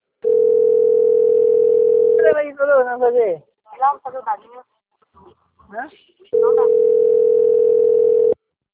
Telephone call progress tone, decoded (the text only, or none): ringback tone